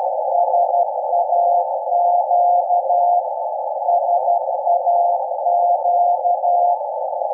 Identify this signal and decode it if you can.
Morse code, heard as a keyed tone, about 12 wpm, 700 Hz, with noise. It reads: SAQ SAQ
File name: rec / 24-Dec-2012-0856 0.020MHz